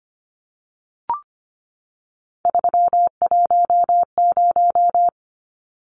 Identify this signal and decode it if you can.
Morse code, heard as a keyed tone, about 25 wpm, 690 Hz. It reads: 310
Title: Numbers - 3-Digit 25wpm